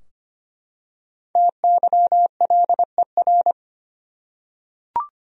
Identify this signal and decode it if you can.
Morse code, heard as a keyed tone, about 25 wpm, 700 Hz. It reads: TYLER
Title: USA Common Names - Max 5 Letters - 25wpm